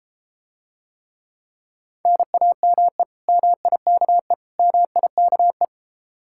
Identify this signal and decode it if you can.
Morse code, heard as a keyed tone, about 34 wpm, 700 Hz. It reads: NAME MIKE MIKE